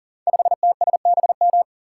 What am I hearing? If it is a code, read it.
Morse code, heard as a keyed tone, about 42 wpm, 700 Hz. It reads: 5TSBM
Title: Code Groups - 5 Letter:Numbers - 40wpm